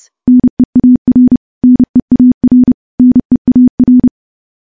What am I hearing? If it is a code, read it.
Morse code, heard as a keyed tone, about 30 wpm, 250 Hz. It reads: NEAR NEAR NEAR
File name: morse-training-30WPM